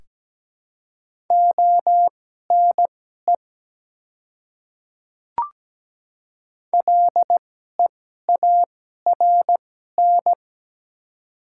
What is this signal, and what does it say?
Morse code, heard as a keyed tone, about 17 wpm, spaced as for 12 wpm, 700 Hz. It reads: ONE LEARN
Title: 2x Character Spacing - Top 300 Words - 17wpm